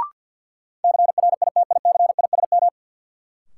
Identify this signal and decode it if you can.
Morse code, heard as a keyed tone, about 52 wpm, 700 Hz. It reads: CRITICISM
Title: Sentences from Top 1300 Words - New Word Drill - 50wpm